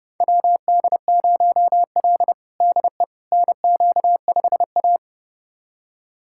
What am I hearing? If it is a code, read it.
Morse code, heard as a keyed tone, about 32 wpm, 700 Hz. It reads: WD0L DE NQ5A
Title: USA Callsign DE Callsign - 4 Characters - 30wpm